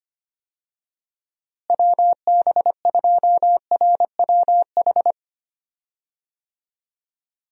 Morse code, heard as a keyed tone, about 25 wpm, 700 Hz. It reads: WB2RWH
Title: USA Call Signs - 6 Characters - Repeated 3x - 25wpm